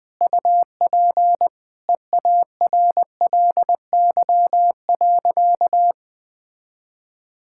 Morse code, heard as a keyed tone, about 21 wpm, 700 Hz. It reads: UP EARLY.
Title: Easy Sentences with Hints - 20wpm